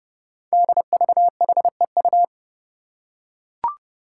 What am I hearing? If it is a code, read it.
Morse code, heard as a keyed tone, about 30 wpm, 700 Hz. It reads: DVHEU
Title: Five Challenging Letter-Number Combinations 30wpm